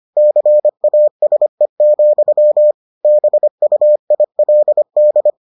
Morse code, heard as a keyed tone, about 25 wpm, 600 Hz. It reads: CASE, BUILD